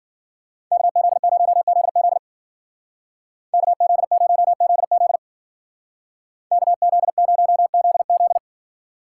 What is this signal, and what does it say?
Morse code, heard as a keyed tone, about 49 wpm, 700 Hz. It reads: KZ0ZZ KZ0ZZ KZ0ZZ
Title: Speed Racing - USA 2x2 Call Signs 40wpm